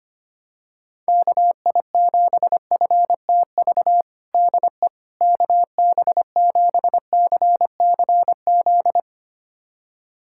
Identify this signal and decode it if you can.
Morse code, heard as a keyed tone, about 25 wpm, 700 Hz. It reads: KI7FTV DE KB7CCZ
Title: USA Callsign DE Callsign - 6 Characters - 25wpm